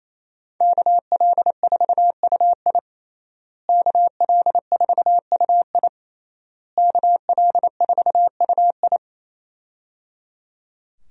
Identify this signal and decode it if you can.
Morse code, heard as a keyed tone, about 27 wpm, 700 Hz. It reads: KL4UI KL4UI KL4UI
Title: USA Call Signs - 5 Characters - Repeated 3x - 28wpm